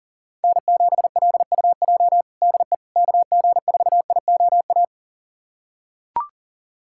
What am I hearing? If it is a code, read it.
Morse code, heard as a keyed tone, about 40 wpm, 700 Hz. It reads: N7LUJ DE KG4IOA
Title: USA Callsign DE Callsign 40wpm